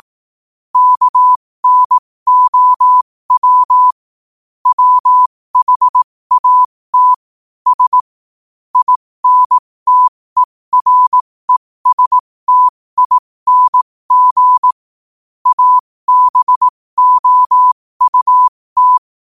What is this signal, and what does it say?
Morse code, heard as a keyed tone, about 18 wpm, spaced as for 15 wpm, 1000 Hz. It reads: KNOW WHAT S INTERESTING ABOUT
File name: qotdfiles / 15wpm